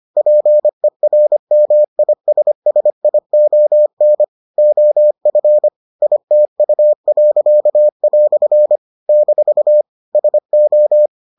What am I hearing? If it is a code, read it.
Morse code, heard as a keyed tone, about 25 wpm, 600 Hz. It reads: PERMISSION OF ITU." - SO